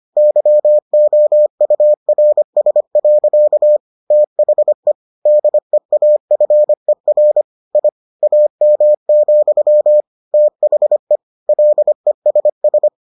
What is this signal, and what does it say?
Morse code, heard as a keyed tone, about 25 wpm, 600 Hz. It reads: YOURS. THE DEAFER I AM, THE LESS